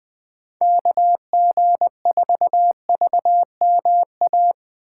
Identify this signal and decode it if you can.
Morse code, heard as a keyed tone, about 21 wpm, 700 Hz. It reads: KG4VMA
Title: USA Call Signs - 6 Characters - 4x Character Spacing - 20wpm